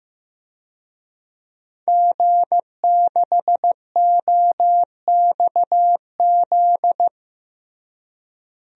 Morse code, heard as a keyed tone, about 15 wpm, 700 Hz. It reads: G6OXZ
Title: DXCC-100 Call Signs - 5 Characters - Repeated 3x - 15wpm